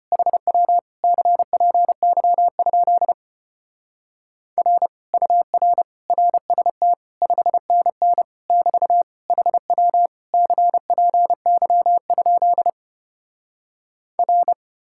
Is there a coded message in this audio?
Yes: Morse code from 0.1 to 14.5 s, heard as a keyed tone, about 33 wpm, 700 Hz. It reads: HW CPY? R UR RST 5NN <BT> HW CPY? R